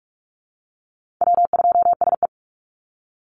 Morse code, heard as a keyed tone, about 44 wpm, 700 Hz. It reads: W2SE
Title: NA SOTA Chaser Callsigns 45wpm